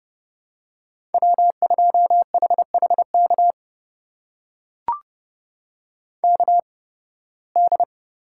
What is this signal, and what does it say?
Morse code, heard as a keyed tone, about 32 wpm, spaced as for 20 wpm, 700 Hz. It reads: W2HHK KD
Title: USA Call Signs - 5 Characters - 8x Character Spacing - 30wpm